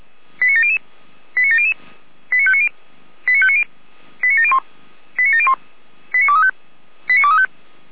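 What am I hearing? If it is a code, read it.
Five-tone selective call (ZVEI), heard as quick runs of five tones, tones of about 70 ms. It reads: 89700 89700 89590 89590 89701 89701 89235 89235
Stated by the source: ZVEI five-tone call received on SDR.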